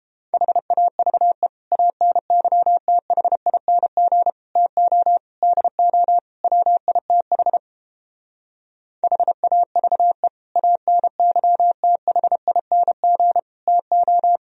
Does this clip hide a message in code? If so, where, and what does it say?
Yes: Morse code from 0.3 to 14.4 s, heard as a keyed tone, about 29 wpm, 700 Hz. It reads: HAVE ANYTHING TO DO WITH HAVE ANYTHING TO